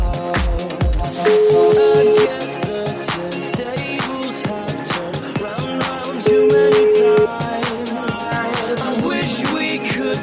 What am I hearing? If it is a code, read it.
Ringback tone (Europe, 425 Hz), with noise.